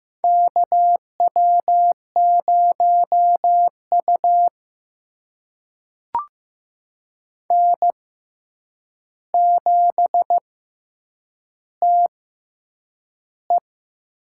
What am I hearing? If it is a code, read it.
Morse code, heard as a keyed tone, about 15 wpm, spaced as for 8 wpm, 700 Hz. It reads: KW0U N7TE
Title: USA Call Signs - 6x Character Spacing - 15wpm